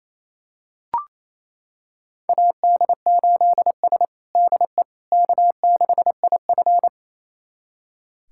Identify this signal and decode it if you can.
Morse code, heard as a keyed tone, about 28 wpm, 700 Hz. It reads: AD8S DE K6IF
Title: USA Callsign DE Callsign - 4 Characters - 28wpm